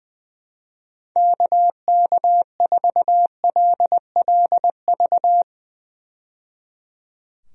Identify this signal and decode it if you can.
Morse code, heard as a keyed tone, about 20 wpm, 700 Hz. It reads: KK4LLV
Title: USA Call Signs - 6 Characters - 20wpm